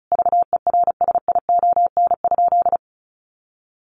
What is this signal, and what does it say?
Morse code, heard as a keyed tone, about 34 wpm, 700 Hz. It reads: VERSION?
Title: Sentences from Top 1400 Words 35wpm